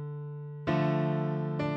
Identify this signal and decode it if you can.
There is no code.